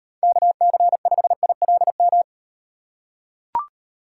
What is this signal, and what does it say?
Morse code, heard as a keyed tone, about 38 wpm, 700 Hz. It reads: KC5ILM